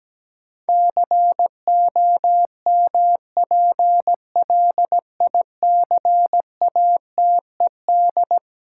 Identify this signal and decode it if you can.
Morse code, heard as a keyed tone, about 17 wpm, 700 Hz. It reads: COMPLICATED
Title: Sentences from Top 1200 Words - New Word Drill Alt - 17wpm